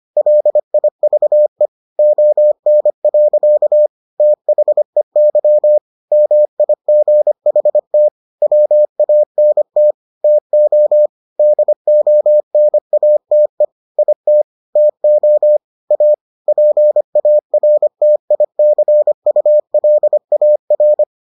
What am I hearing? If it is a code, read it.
Morse code, heard as a keyed tone, about 25 wpm, 600 Hz. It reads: LIVE ON. THEY MIGHT WANT TO DONATE IT TO A PARTICULAR